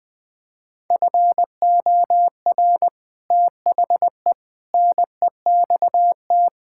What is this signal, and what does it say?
Morse code, heard as a keyed tone, about 20 wpm, 700 Hz. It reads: FOR THE NEXT
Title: Sentences with 3-Word Phrases - Spoken Beginning - 20wpm